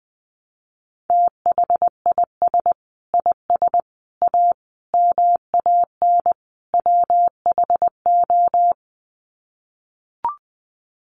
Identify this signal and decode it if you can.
Morse code, heard as a keyed tone, about 20 wpm, 700 Hz. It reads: THIS IS A MAN WHO
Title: Sets of 5 Words Encore2 20wpm